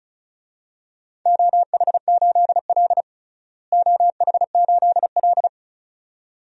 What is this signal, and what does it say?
Morse code, heard as a keyed tone, about 34 wpm, 700 Hz. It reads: OH8L OH8L
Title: DXCC-100 Call Signs - 4 Characters - Repeated 2x - 35wpm